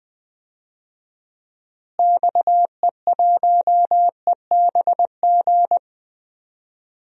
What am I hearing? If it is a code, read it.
Morse code, heard as a keyed tone, about 20 wpm, 700 Hz. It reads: XE1EBG
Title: DXCC-100 Call Signs - 6 Characters - Repeated 2x - 20wpm